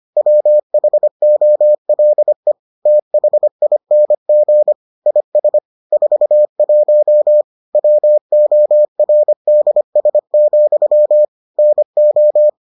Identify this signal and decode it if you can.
Morse code, heard as a keyed tone, about 25 wpm, 600 Hz. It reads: WHOLE THING IS 41 WORDS, NO